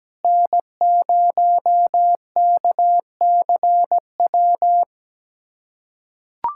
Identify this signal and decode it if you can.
Morse code, heard as a keyed tone, about 17 wpm, 700 Hz. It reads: N0KCW
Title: USA Callsign DE Callsign - 5 Characters - 17wpm